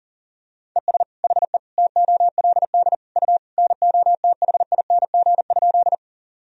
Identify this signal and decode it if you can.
Morse code, heard as a keyed tone, about 40 wpm, 700 Hz. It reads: ES HE TOLD U NOTHING?
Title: Sentences from Top 500 Words and Abbreviations 40wpm